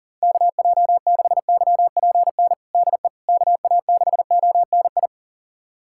Morse code, heard as a keyed tone, about 37 wpm, 690 Hz. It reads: KJ6YPN DE KA6ONI
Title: USA Callsign DE Callsign - 6 Characters - 40wpm